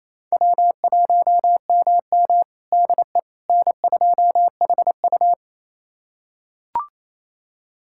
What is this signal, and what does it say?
Morse code, heard as a keyed tone, about 28 wpm, 700 Hz. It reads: W1MM DE N2HU
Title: USA Callsign DE Callsign - 4 Characters - 28wpm